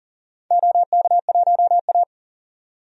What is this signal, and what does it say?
Morse code, heard as a keyed tone, about 38 wpm, 700 Hz. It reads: OK1A
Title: DXCC-100 Call Signs - 4 Characters - 40wpm